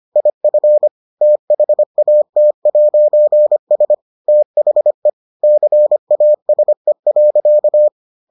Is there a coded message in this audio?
Yes: Morse code from 0.2 to 7.9 s, heard as a keyed tone, about 25 wpm, 600 Hz. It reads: IF THAT'S THE CASE.